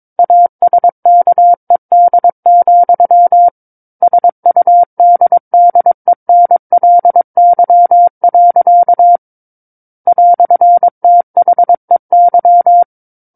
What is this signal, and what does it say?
Morse code, heard as a keyed tone, about 22 wpm, 700 Hz. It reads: ASKED, SUDDENLY. "THEY
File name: Book One / 13-Chapter 13 - 22 WPM